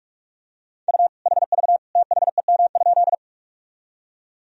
Morse code, heard as a keyed tone, about 46 wpm, 700 Hz. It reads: U HV THEM?
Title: Sentences from Top 100 Words and Abbreviations 45wpm